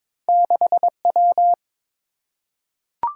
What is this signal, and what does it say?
Morse code, heard as a keyed tone, about 22 wpm, 700 Hz. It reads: 6W
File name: practice / 22wpm / Two Letter-Number Combinations - 22wpm